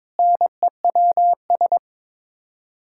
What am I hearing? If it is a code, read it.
Morse code, heard as a keyed tone, about 22 wpm, 700 Hz. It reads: NEWS
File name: Sets of 9 Words - 22wpm